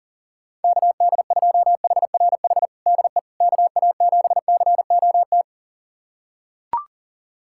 Morse code, heard as a keyed tone, about 40 wpm, 700 Hz. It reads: KD2HRH DE KA7COT
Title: USA Callsign DE Callsign 40wpm